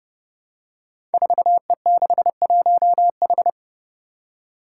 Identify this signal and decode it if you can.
Morse code, heard as a keyed tone, about 30 wpm, 700 Hz. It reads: 4E61H
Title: Five Challenging Letter-Number Combinations 30wpm